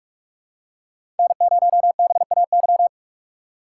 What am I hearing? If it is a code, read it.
Morse code, heard as a keyed tone, about 45 wpm, 700 Hz. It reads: N0BAY
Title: USA Call Signs - 5 Characters - 4x Character Spacing - 45wpm